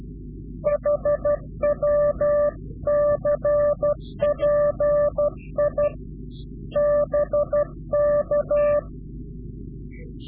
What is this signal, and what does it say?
Morse code, heard as a keyed tone, about 12 wpm, 590 Hz. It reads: HW CPI BK
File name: CW-QSO / Directional Call / Directional QSO